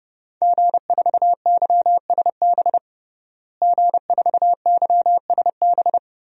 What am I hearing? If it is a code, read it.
Morse code, heard as a keyed tone, about 30 wpm, 700 Hz. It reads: G4YSB G4YSB
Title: DXCC-100 Call Signs - 5 Characters - Repeated 2x - 30wpm